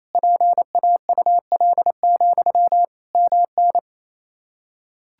Morse code, heard as a keyed tone, about 27 wpm, 700 Hz. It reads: PAUL, MN
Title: Practice QSO - Exchange 1 - Beginning - 28wpm